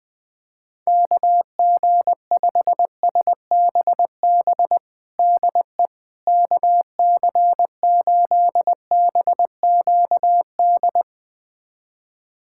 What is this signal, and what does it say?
Morse code, heard as a keyed tone, about 20 wpm, 700 Hz. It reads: KG5SBB DE KC8BQD
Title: USA Callsign DE Callsign - 6 Characters - 20wpm